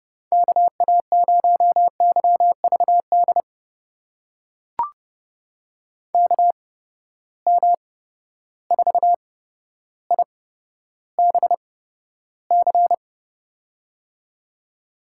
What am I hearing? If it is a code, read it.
Morse code, heard as a keyed tone, about 30 wpm, spaced as for 13 wpm, 700 Hz. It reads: KA0YVD KM4IBC
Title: USA Call Signs - 6 Characters - 8x Character Spacing - 30wpm